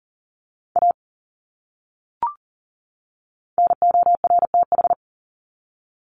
Morse code, heard as a keyed tone, about 40 wpm, 700 Hz. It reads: A NORTH
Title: Top 300 Words 40wpm